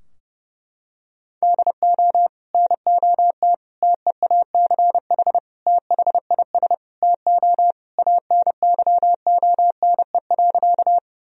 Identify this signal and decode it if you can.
Morse code, heard as a keyed tone, about 30 wpm, 700 Hz. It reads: DO NOT TEACH THIS TO ANYONE.
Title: Sentences from Top 1000 Words - Review Part 2 - 30wpm